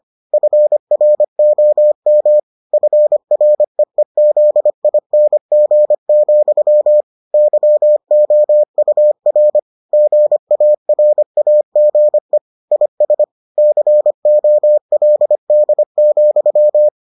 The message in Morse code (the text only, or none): FROM FREEZING, YOUR GARAGE IS COLD,